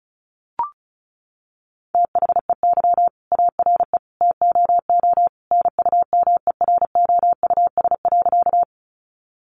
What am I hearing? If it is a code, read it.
Morse code, heard as a keyed tone, about 34 wpm, 700 Hz. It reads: THEY ARE TOO NUMEROUS.
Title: Sentences from Top 1400 Words 35wpm